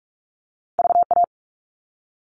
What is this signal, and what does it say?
Morse code, heard as a keyed tone, about 44 wpm, 700 Hz. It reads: VA
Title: ARRL Sections 45wpm